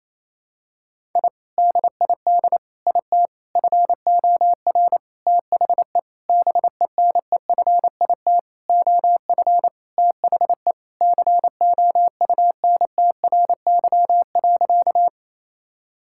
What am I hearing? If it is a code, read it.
Morse code, heard as a keyed tone, about 28 wpm, 700 Hz. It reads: I DID IT FOR THE BENEFIT OF THE COUNTRY.
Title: Sentences from Top 1000 Words - Review Part 6 - 28wpm